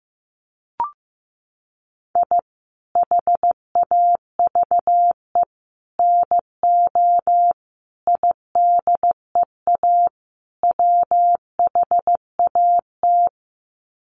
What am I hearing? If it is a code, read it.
Morse code, heard as a keyed tone, about 15 wpm, 700 Hz. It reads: I HAVE NO IDEA WHAT
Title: Sets of 5 Words Encore 15wpm